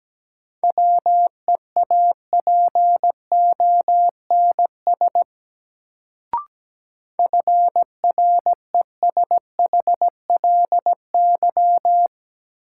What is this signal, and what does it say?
Morse code, heard as a keyed tone, about 17 wpm, 700 Hz. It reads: WEAPONS FRESHLY
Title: Sets of 5 Words Encore - New Word Drill - 17wpm